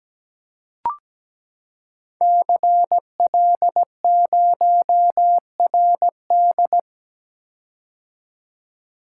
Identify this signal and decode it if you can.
Morse code, heard as a keyed tone, about 17 wpm, 700 Hz. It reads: CL0RD
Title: Code Groups - 5 Letter:Numbers - 17wpm